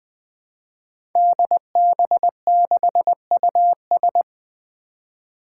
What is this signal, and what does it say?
Morse code, heard as a keyed tone, about 20 wpm, 700 Hz. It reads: DB6US
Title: Five Challenging Letter-Number Combinations 20wpm